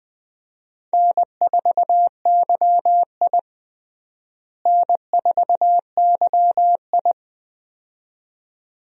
Morse code, heard as a keyed tone, about 20 wpm, 700 Hz. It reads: N4YI N4YI
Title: USA Call Signs - 4 Characters - Repeated 2x - 20wpm